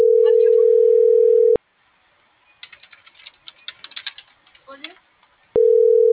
Ringback tone (North America), 440 + 480 Hz.
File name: CENTRAL FRALSE / out aidee.cruz